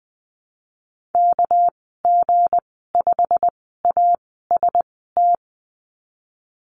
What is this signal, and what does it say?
Morse code, heard as a keyed tone, about 20 wpm, spaced as for 15 wpm, 700 Hz. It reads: KG5AST
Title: 238 Lesson 38g - Letter q - Call Signs 20wpm